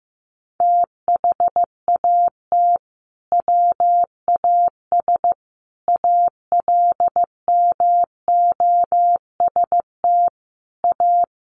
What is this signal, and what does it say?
Morse code, heard as a keyed tone, about 15 wpm, 700 Hz. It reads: THAT WAS ALMOST A
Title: Sentences from Top 1400 Words 15wpm